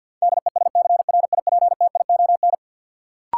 Morse code, heard as a keyed tone, about 53 wpm, 700 Hz. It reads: DESCRIPTION
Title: Sentences from Top 1300 Words - New Word Drill - 50wpm